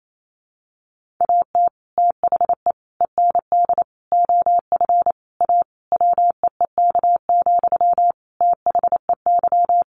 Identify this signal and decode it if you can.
Morse code, heard as a keyed tone, about 28 wpm, 700 Hz. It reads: AT THE END OF A WEEK, THEY
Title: Sentences from Top 1400 Words 28wpm